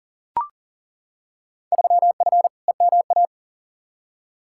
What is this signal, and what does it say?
Morse code, heard as a keyed tone, about 40 wpm, 700 Hz. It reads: 3F EMA
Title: Field Day Exchanges Encore - 40wpm